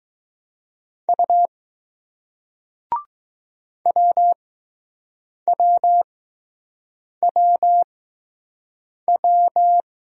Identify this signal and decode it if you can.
Morse code, heard as a keyed tone, about 20 wpm, spaced as for 6 wpm, 700 Hz. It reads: U WWWW